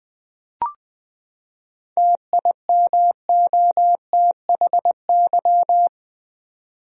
Morse code, heard as a keyed tone, about 20 wpm, 700 Hz. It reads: TIMOTHY
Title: USA Common Names - 20wpm